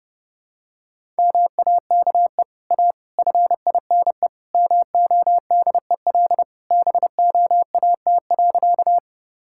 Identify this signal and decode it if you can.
Morse code, heard as a keyed tone, about 29 wpm, 700 Hz. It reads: MAKE A FINE MODEL BOAT.
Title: Sentences from Top 1000 Words - Review Part 5 - 30wpm